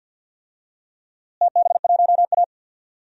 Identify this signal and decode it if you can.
Morse code, heard as a keyed tone, about 53 wpm, 700 Hz. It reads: TB1A
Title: DXCC-100 Call Signs - 4 Characters - Repeated 3x - 50wpm